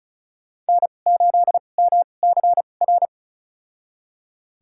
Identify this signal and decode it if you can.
Morse code, heard as a keyed tone, about 36 wpm, spaced as for 25 wpm, 700 Hz. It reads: N8MCR
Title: USA Call Signs - 5 Characters - 2x Character Spacing - 35wpm